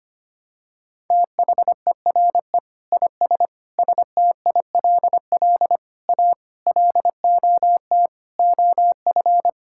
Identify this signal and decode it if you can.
Morse code, heard as a keyed tone, about 25 wpm, 700 Hz. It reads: THERE IS STILL A LOT OF
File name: Sets of 9 Words - 25wpm